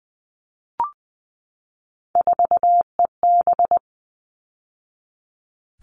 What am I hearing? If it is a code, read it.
Morse code, heard as a keyed tone, about 20 wpm, 700 Hz. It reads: 4EB
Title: All Cut Numbers - 3-Digit 20wpm